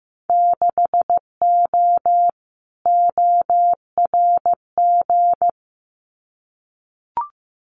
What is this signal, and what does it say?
Morse code, heard as a keyed tone, about 15 wpm, 700 Hz. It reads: 6O ORG
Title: Winter Field Day Exchanges Encore 15wpm